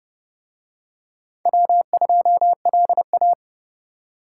Morse code, heard as a keyed tone, about 28 wpm, 700 Hz. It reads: W2LA